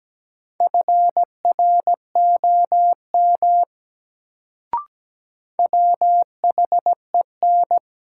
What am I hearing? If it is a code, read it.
Morse code, heard as a keyed tone, about 17 wpm, 700 Hz. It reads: FROM WHEN
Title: Sentences from Top 100 Words - New Word Drill - 17wpm